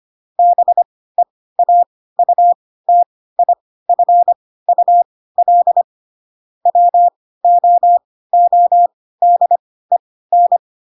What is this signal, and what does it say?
Morse code, heard as a keyed tone, about 25 wpm, spaced as for 16 wpm, 700 Hz. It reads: BEAUTIFUL WOODEN
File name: Pinocchio / Pinocchio ch2 10 25 ik1hge